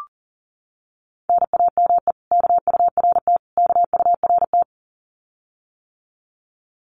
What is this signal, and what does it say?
Morse code, heard as a keyed tone, about 40 wpm, 700 Hz. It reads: NAME KURT KURT
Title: Common QSO Phrases Encore 40wpm